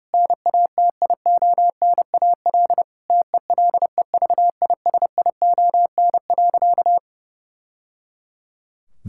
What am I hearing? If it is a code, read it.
Morse code, heard as a keyed tone, about 32 wpm, 700 Hz. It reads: NATIONAL TELEVISION.
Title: Sentences from Top 1000 Words - Review Part 5 - 30wpm